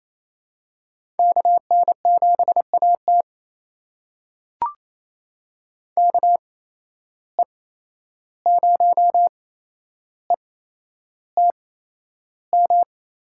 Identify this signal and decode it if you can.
Morse code, heard as a keyed tone, about 27 wpm, spaced as for 11 wpm, 700 Hz. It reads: KN7AT KE0ETM
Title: USA Call Signs - 8x Character Spacing - 28wpm